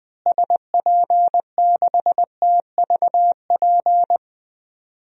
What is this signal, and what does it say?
Morse code, heard as a keyed tone, about 20 wpm, 700 Hz. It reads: SP6TVP
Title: DXCC-100 Call Signs - 6 Characters - Repeated 2x - 20wpm